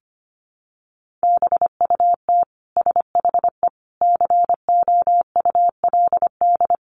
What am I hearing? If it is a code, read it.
Morse code, heard as a keyed tone, about 25 wpm, 700 Hz. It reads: BUT SHE COULD
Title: Sentences from Top 600 Words 25wpm